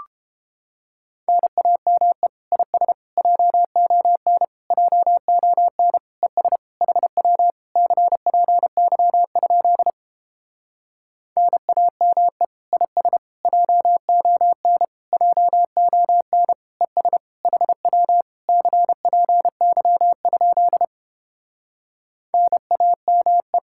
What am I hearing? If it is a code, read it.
Morse code, heard as a keyed tone, about 29 wpm, 700 Hz. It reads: NAME IS JON JON ES HW CPY? NAME IS JON JON ES HW CPY? NAME